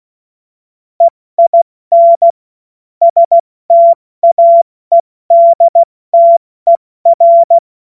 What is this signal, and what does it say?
Morse code, heard as a keyed tone, about 16 wpm, spaced as for 14 wpm, 680 Hz. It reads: EIN STAEDTER
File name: Morsekurs / Witze / Witz 24 Juni Mai 2025 70 F80